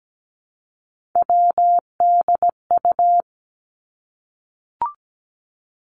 Morse code, heard as a keyed tone, about 17 wpm, 700 Hz. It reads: WDU
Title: Three Challenging Letter-Number Combinations 17wpm